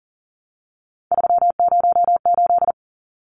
Morse code, heard as a keyed tone, about 40 wpm, 700 Hz. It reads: 308
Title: Numbers - 3-Digit 40wpm